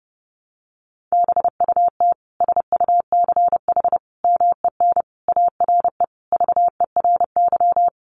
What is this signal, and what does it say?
Morse code, heard as a keyed tone, about 30 wpm, 700 Hz. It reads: BUT SUCH MEN ARE VERY
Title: Sentences from Top 1100 Words 30wpm